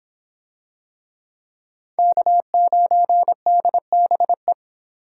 Morse code, heard as a keyed tone, about 25 wpm, 700 Hz. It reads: K9DBE